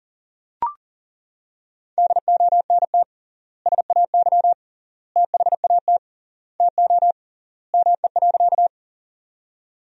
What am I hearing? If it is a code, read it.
Morse code, heard as a keyed tone, about 37 wpm, 700 Hz. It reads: DONT SAY THAT TO ME.
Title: 3x Word-Spacing - Sentences from Top 300 Words - 40wpm